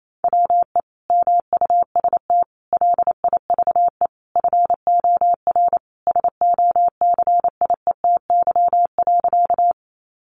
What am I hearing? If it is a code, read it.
Morse code, heard as a keyed tone, about 28 wpm, 700 Hz. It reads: WE MUST LIVE FOR SOCIETY.
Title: Sentences from Top 700 Words 28wpm